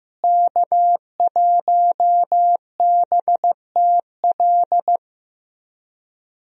Morse code, heard as a keyed tone, about 15 wpm, 700 Hz. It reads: K1BTL
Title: USA Call Signs - 5 Characters - Repeated 2x - 15wpm